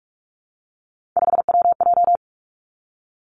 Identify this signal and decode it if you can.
Morse code, heard as a keyed tone, about 46 wpm, 700 Hz. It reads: 5WJ
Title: Three Challenging Letter-Number Combinations 45wpm